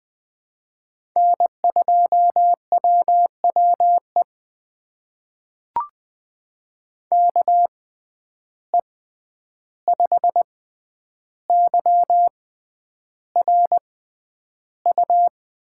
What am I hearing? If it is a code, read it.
Morse code, heard as a keyed tone, about 20 wpm, spaced as for 10 wpm, 700 Hz. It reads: N2WWE KE5YRU
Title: USA Call Signs - 6x Character Spacing - 20wpm